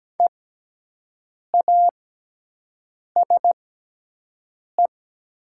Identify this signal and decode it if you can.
Morse code, heard as a keyed tone, about 17 wpm, spaced as for 6 wpm, 700 Hz. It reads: EASE
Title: 6x Character Spacing - Top 500 Words - 17wpm